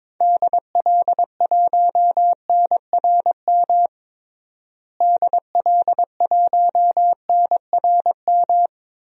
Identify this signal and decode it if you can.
Morse code, heard as a keyed tone, about 22 wpm, 700 Hz. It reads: DL1NRM DL1NRM
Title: DXCC-100 Call Signs - 6 Characters - Repeated 2x - 22wpm